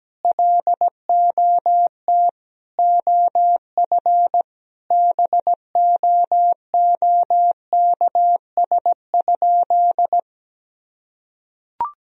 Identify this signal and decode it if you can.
Morse code, heard as a keyed tone, about 17 wpm, 700 Hz. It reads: LOT OF BOOKS?
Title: Sentences from Top 1000 Words - Review Part 4 - 17wpm